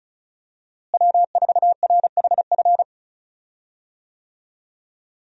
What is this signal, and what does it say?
Morse code, heard as a keyed tone, about 36 wpm, 700 Hz. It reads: W4RHF
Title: USA Call Signs - 5 Characters - Repeated 3x - 35wpm